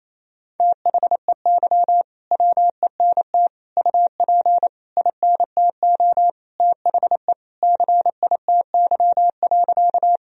Morse code, heard as a keyed tone, about 28 wpm, 700 Hz. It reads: THEY WENT UP INTO THE CITY.
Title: Sentences from Top 1000 Words - Review Part 1 - 28wpm